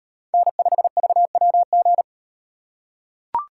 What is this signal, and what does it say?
Morse code, heard as a keyed tone, about 33 wpm, 700 Hz. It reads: N5VWG